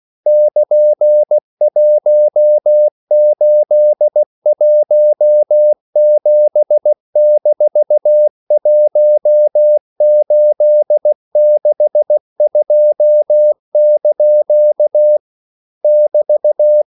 Morse code, heard as a keyed tone, about 16 wpm, 600 Hz. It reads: <KN>1817-1862) <BT>
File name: CW / Practice / EnglishQuotations0216